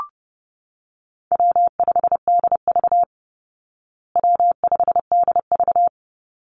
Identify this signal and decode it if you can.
Morse code, heard as a keyed tone, about 30 wpm, 700 Hz. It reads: W5DV W5DV
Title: Easy Call Signs Repeated 3x - 30wpm